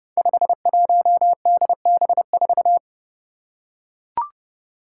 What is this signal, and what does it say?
Morse code, heard as a keyed tone, about 30 wpm, 700 Hz. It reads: 51DB4
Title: Five Challenging Letter-Number Combinations 30wpm